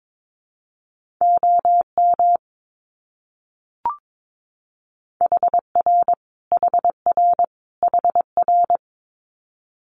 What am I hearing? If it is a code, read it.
Morse code, heard as a keyed tone, about 22 wpm, 700 Hz. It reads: OM HR HR HR
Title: Abbreviations Repeated 3x - 22wpm